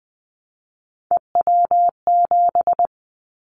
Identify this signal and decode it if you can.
Morse code, heard as a keyed tone, about 20 wpm, 700 Hz. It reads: EW7
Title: Callsign Prefixes for Top 100 DXCC - 20wpm